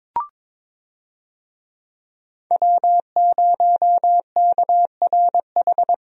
Morse code, heard as a keyed tone, about 21 wpm, 700 Hz. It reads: W0KRH
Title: USA Call Signs - 5 Characters - Repeated 2x - 22wpm